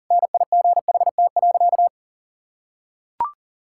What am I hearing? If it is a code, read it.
Morse code, heard as a keyed tone, about 40 wpm, 700 Hz. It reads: NIGHT.
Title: Sentences from Top 1000 Words - Review Part 6 - 40wpm